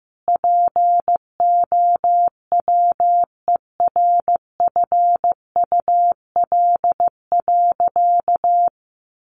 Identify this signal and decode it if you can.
Morse code, heard as a keyed tone, about 15 wpm, 700 Hz. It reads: POWERFUL.
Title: Sentences from Top 1300 Words 15wpm